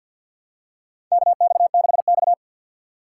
Morse code, heard as a keyed tone, about 52 wpm, 700 Hz. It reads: KX6X
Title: K1USN SST QSO Practice - 50wpm